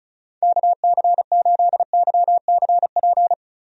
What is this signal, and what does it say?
Morse code, heard as a keyed tone, about 34 wpm, 700 Hz. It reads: KC8YCP
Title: USA Call Signs - 6 Characters - 35wpm